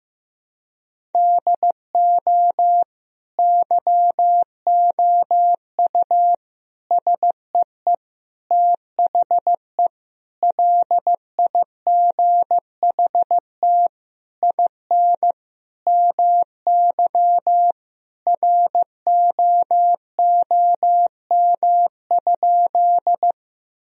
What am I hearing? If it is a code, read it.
Morse code, heard as a keyed tone, about 16 wpm, 700 Hz. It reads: DO YOU SEE THE LIGHT IN MY ROOM?
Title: Sentences from Top 1000 Words - Review Part 1 - 15wpm